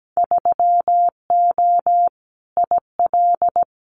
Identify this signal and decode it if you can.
Morse code, heard as a keyed tone, about 17 wpm, 700 Hz. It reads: 3O IL
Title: Winter Field Day Exchanges 17wpm